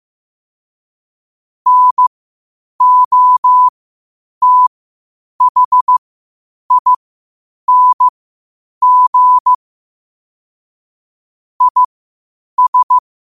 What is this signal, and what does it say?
Morse code, heard as a keyed tone, about 15 wpm, spaced as for 8 wpm, 1000 Hz. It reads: NOTHING IS
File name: qotdfiles / 8wpm